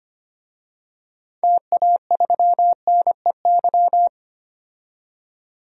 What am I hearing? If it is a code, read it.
Morse code, heard as a keyed tone, about 25 wpm, 700 Hz. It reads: TA3NEY
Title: DXCC-100 Call Signs - 6 Characters - 25wpm